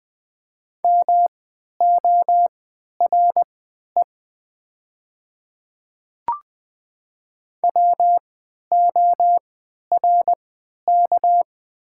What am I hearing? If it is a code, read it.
Morse code, heard as a keyed tone, about 20 wpm, spaced as for 11 wpm, 700 Hz. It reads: MORE WORK
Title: 3x Character Spacing - Top 200 Words - 20wpm